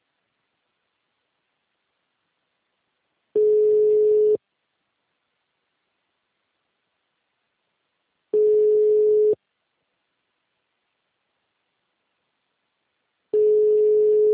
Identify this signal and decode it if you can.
Ringback tone (Europe, 425 Hz).